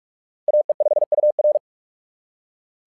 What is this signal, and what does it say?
Morse code, heard as a keyed tone, about 44 wpm, 600 Hz. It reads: AE5UR
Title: SOTA QSO Practice 45wpm